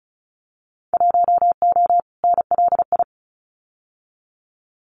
Morse code, heard as a keyed tone, about 34 wpm, 700 Hz. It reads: 1O NLI
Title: Winter Field Day Exchanges 35wpm